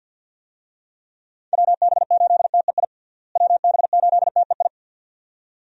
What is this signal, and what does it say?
Morse code, heard as a keyed tone, about 52 wpm, 700 Hz. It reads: WB8TEI WB8TEI
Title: USA Call Signs - 6 Characters - Repeated 2x - 50wpm